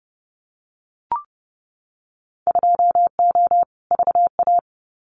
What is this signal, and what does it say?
Morse code, heard as a keyed tone, about 30 wpm, 700 Hz. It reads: 2O VA
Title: Winter Field Day Exchanges 30wpm